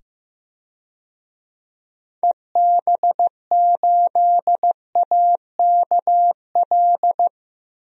Morse code, heard as a keyed tone, about 15 wpm, 700 Hz. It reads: EB8AKL
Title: DXCC-100 Call Signs - 6 Characters - 15wpm